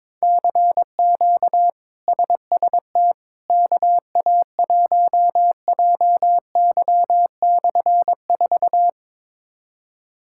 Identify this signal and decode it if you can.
Morse code, heard as a keyed tone, about 22 wpm, 700 Hz. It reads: CQ SST KA1JY/4
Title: K1USN SST QSO Practice - 22wpm